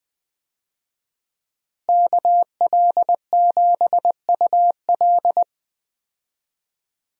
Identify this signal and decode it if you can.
Morse code, heard as a keyed tone, about 20 wpm, 700 Hz. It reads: KL7UL
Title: USA Call Signs - 5 Characters - Repeated 3x - 20wpm